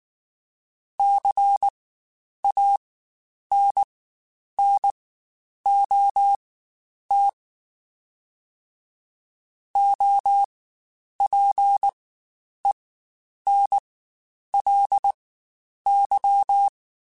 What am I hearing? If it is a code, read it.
Morse code, heard as a keyed tone, about 19 wpm, spaced as for 9 wpm, 800 Hz. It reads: CANNOT OPENLY
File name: Quotes Spaced / Quotes Spaced-mp3s / Advice sp @18